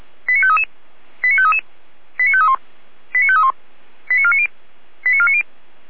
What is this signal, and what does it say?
Five-tone selective call (ZVEI), heard as quick runs of five tones, tones of about 70 ms. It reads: 89533 89533 89531 89531 89490 89490